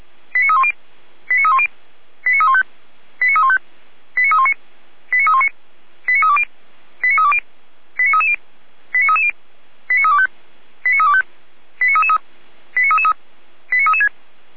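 Five-tone selective call (ZVEI), heard as quick runs of five tones, tones of about 70 ms. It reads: 89310 89310 89315 89315 89319 89319 89320 89320 89330 89330 89325 89325 89333 89336
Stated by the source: ZVEI five-tone call received by SDR, archived by Feuerwehr Orlamünde.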